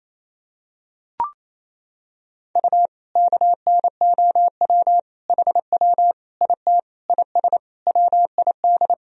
Morse code, heard as a keyed tone, about 28 wpm, 700 Hz. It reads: U KNOW HW IT IS WID
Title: Sentences from Top 100 Words and Abbreviations 28wpm